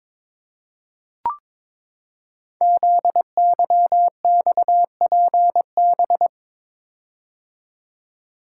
Morse code, heard as a keyed tone, about 22 wpm, 700 Hz. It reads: ZYXPB
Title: Code Groups - 5 Letter:Numbers - 22wpm